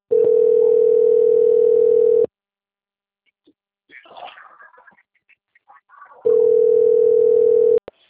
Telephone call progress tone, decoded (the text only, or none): ringback tone